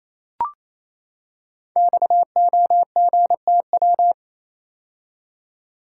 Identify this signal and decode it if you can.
Morse code, heard as a keyed tone, about 27 wpm, 700 Hz. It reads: XOGTW
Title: Code Groups - 5 Letter:Numbers - 28wpm